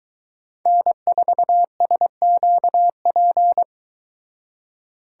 Morse code, heard as a keyed tone, about 23 wpm, 700 Hz. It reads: N4SQP